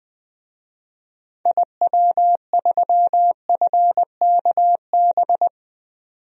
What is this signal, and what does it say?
Morse code, heard as a keyed tone, about 20 wpm, 700 Hz. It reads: IW3FKB
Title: DXCC-100 Call Signs - 6 Characters - 20wpm